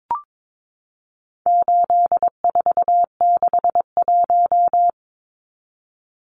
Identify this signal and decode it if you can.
Morse code, heard as a keyed tone, about 22 wpm, 700 Hz. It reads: 8461
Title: Numbers - 4-Digit 22wpm